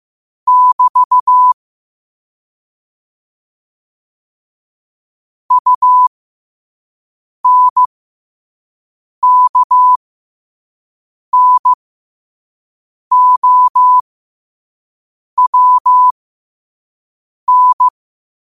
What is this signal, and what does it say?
Morse code, heard as a keyed tone, about 15 wpm, spaced as for 5 wpm, 1000 Hz. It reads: <BT> UNKNOWN
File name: qotdfiles / 5wpm